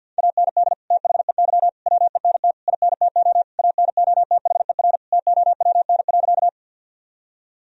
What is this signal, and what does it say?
Morse code, heard as a keyed tone, about 52 wpm, 700 Hz. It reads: AND THEY WENT INTO ANOTHER TOWN.
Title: Sentences from Top 1000 Words - Review Part 2 - 50wpm